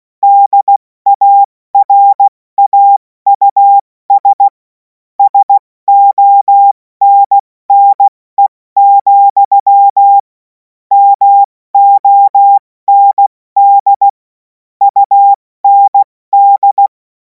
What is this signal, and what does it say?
Morse code, heard as a keyed tone, about 16 wpm, 800 Hz. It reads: DARAUS SONNE, MOND UND
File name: Morsekurs / Witze / Witz 24. Februar 2026 70 F80